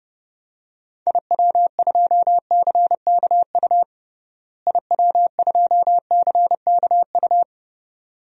Morse code, heard as a keyed tone, about 30 wpm, 700 Hz. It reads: IW2CKU IW2CKU
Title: DXCC-100 Call Signs - 6 Characters - Repeated 2x - 30wpm